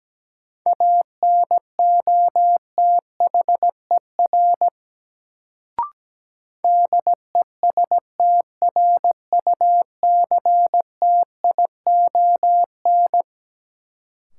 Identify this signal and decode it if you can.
Morse code, heard as a keyed tone, about 17 wpm, 700 Hz. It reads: ANOTHER DESTRUCTION
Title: Sets of 5 Words Encore - New Word Drill - 17wpm